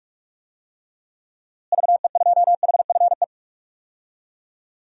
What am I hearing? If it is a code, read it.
Morse code, heard as a keyed tone, about 41 wpm, 700 Hz. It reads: VE2HFE
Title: DXCC-100 Call Signs - 6 Characters - Repeated 2x - 45wpm